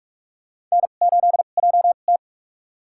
Morse code, heard as a keyed tone, about 44 wpm, spaced as for 32 wpm, 700 Hz. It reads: N8JT
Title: USA Call Signs - 4 Characters - 2x Character Spacing - 45wpm